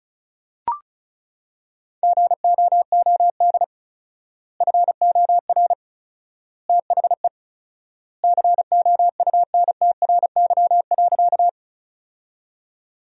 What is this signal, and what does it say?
Morse code, heard as a keyed tone, about 36 wpm, 700 Hz. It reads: GOOD FOR THE COUNTRY.
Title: 4x Word-Spacing - Sentences from Top 300 Words - 35wpm